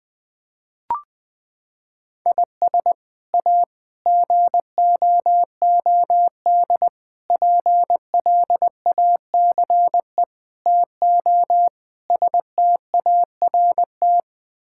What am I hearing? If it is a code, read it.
Morse code, heard as a keyed tone, about 20 wpm, 700 Hz. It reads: IS A GOOD PLACE TO START
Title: Sets of 6 Words - 20wpm